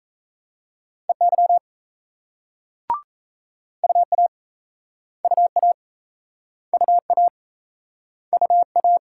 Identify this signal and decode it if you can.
Morse code, heard as a keyed tone, about 38 wpm, 700 Hz. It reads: EY UA UA UA UA